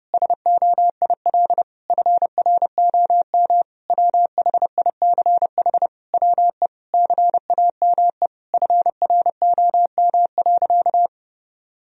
Morse code, heard as a keyed tone, about 30 wpm, 700 Hz. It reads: SOIL FROM WHICH WE CAME FROM.
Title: Sentences from Top 1000 Words - Review Part 4 - 30wpm